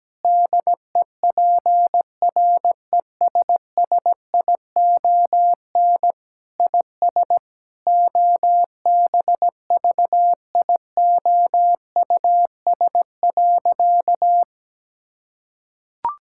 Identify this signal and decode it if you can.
Morse code, heard as a keyed tone, about 17 wpm, 700 Hz. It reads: DEPRESSION IS OBVIOUS.
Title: Sentences from Top 1600 Words 17wpm